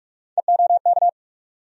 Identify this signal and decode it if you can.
Morse code, heard as a keyed tone, about 46 wpm, 700 Hz. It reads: EOK
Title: USA Call Sign Postfixes - 3 Letters 45wpm